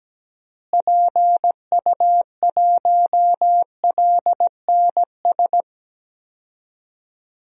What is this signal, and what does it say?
Morse code, heard as a keyed tone, about 17 wpm, 700 Hz. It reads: PU1LNS
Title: DXCC-100 Call Signs - 6 Characters - Repeated 3x - 17wpm